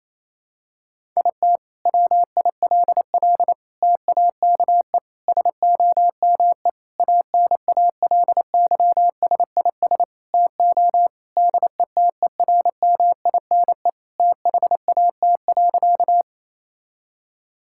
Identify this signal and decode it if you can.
Morse code, heard as a keyed tone, about 28 wpm, 700 Hz. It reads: IT WILL TAKE SOME ANALYSIS TO DETERMINE THAT.
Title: Sentences from Top 1000 Words - Review Part 5 - 28wpm